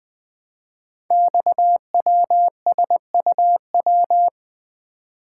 Morse code, heard as a keyed tone, about 20 wpm, 700 Hz. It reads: XWSUW
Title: Code Groups - 5 Letters - 20wpm